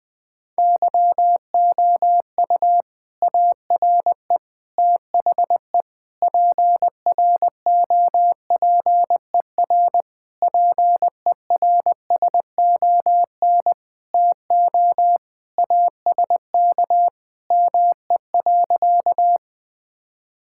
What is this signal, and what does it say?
Morse code, heard as a keyed tone, about 21 wpm, 700 Hz. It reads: YOU ARE THE PROPER PERSON TO ASK ME.
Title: Sentences from Top 1000 Words - Review Part 5 - 20wpm